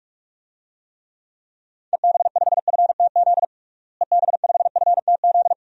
Morse code, heard as a keyed tone, about 42 wpm, 700 Hz. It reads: EB5FTZ EB5FTZ
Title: DXCC-100 Call Signs - 6 Characters - Repeated 2x - 45wpm